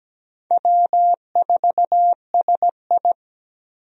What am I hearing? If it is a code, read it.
Morse code, heard as a keyed tone, about 17 wpm, 700 Hz. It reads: W4SI
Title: USA Call Signs - 4 Characters - 17wpm